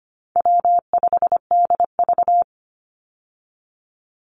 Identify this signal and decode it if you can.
Morse code, heard as a keyed tone, about 25 wpm, 700 Hz. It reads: W5DV
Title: Easy Call Signs - 25wpm